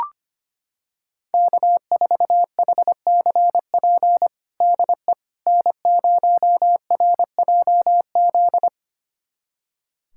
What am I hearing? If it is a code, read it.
Morse code, heard as a keyed tone, about 25 wpm, 700 Hz. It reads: K4HCP DE N0RJZ
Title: USA Callsign DE Callsign - 5 Characters - 25wpm